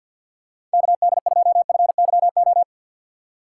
Morse code, heard as a keyed tone, about 49 wpm, 700 Hz. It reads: KD2FYO
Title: USA Call Signs - 4x Character Spacing - 50wpm